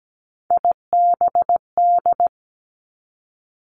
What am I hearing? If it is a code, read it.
Morse code, heard as a keyed tone, about 17 wpm, 700 Hz. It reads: IBD
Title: Three Challenging Letter-Number Combinations 17wpm